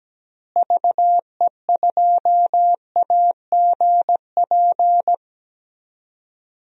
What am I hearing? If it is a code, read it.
Morse code, heard as a keyed tone, about 17 wpm, 700 Hz. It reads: VE2AGP
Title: DXCC-100 Call Signs - 6 Characters - Repeated 2x - 17wpm